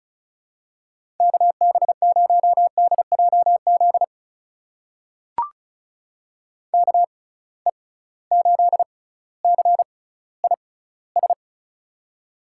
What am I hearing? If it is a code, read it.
Morse code, heard as a keyed tone, about 36 wpm, spaced as for 18 wpm, 690 Hz. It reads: KB0DJZ KE8CIS
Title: USA Call Signs - 6 Characters - 6x Character Spacing - 35wpm